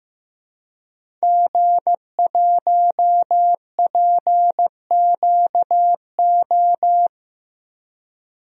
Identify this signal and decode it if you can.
Morse code, heard as a keyed tone, about 15 wpm, 700 Hz. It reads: G1PQO
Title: DXCC-100 Call Signs - 5 Characters - Repeated 3x - 15wpm